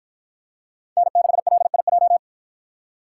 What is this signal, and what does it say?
Morse code, heard as a keyed tone, about 53 wpm, 700 Hz. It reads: N6LIJ